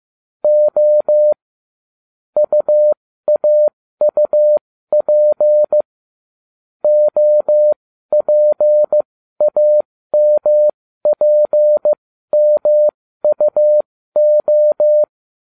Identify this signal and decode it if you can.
Morse code, heard as a keyed tone, about 15 wpm, spaced as for 13 wpm, 600 Hz. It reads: O UAUP OPAMPMUO